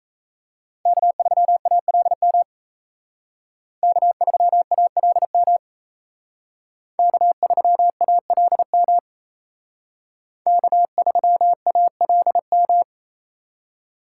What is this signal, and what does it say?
Morse code, heard as a keyed tone, about 35 wpm, 700 Hz. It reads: K3ALM K3ALM K3ALM K3ALM
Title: Speed Racing - USA 1x3 Call Signs 28wpm